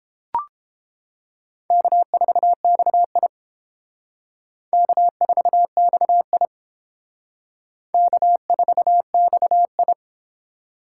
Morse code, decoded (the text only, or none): K4XI K4XI K4XI